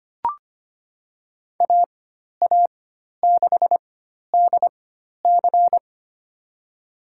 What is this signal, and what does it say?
Morse code, heard as a keyed tone, about 25 wpm, spaced as for 12 wpm, 700 Hz. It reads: AA6DC
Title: USA Call Signs - 5 Characters - 4x Character Spacing - 25wpm